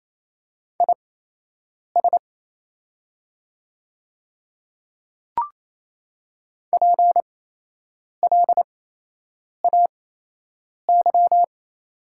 Morse code, heard as a keyed tone, about 27 wpm, spaced as for 8 wpm, 700 Hz. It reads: IS PLAY
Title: 8x Character Spacing - Top 300 Words - 28wpm